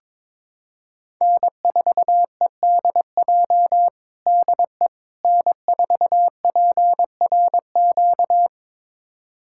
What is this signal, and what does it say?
Morse code, heard as a keyed tone, about 22 wpm, 700 Hz. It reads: N4EDJ DE N4PRQ
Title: USA Callsign DE Callsign - 5 Characters - 22wpm